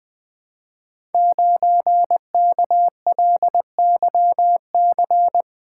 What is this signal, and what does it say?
Morse code, heard as a keyed tone, about 20 wpm, 700 Hz. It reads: 9KLYC
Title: Code Groups - 5 Letter:Numbers - 20wpm